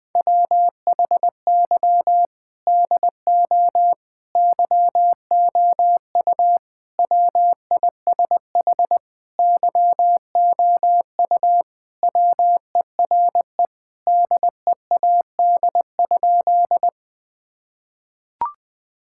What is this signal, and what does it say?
Morse code, heard as a keyed tone, about 21 wpm, 700 Hz. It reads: WHY DO YOU WISH YOU WERE DEAD?
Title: Sentences from Top 1600 Words 20wpm